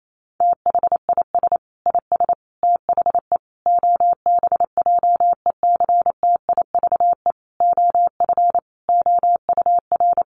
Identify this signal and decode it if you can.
Morse code, heard as a keyed tone, about 28 wpm, 700 Hz. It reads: THIS IS THE OBJECTIVE OF OUR
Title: Sentences from Top 1200 Words 28wpm